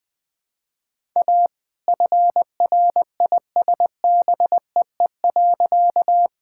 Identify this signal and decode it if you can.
Morse code, heard as a keyed tone, about 21 wpm, 700 Hz. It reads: A FRISBEE.
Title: All-Age Appropriate Jokes - 20wpm